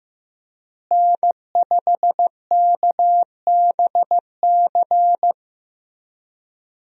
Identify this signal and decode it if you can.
Morse code, heard as a keyed tone, about 15 wpm, 700 Hz. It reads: N5KBC
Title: USA Call Signs - 5 Characters - Repeated 3x - 15wpm